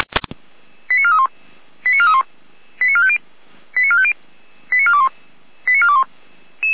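Five-tone selective call (ZVEI), heard as quick runs of five tones, tones of about 70 ms. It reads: 89431 89431 89450 89450 89421 89421
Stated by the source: ZVEI five-tone selective call 89403 received on an SDR.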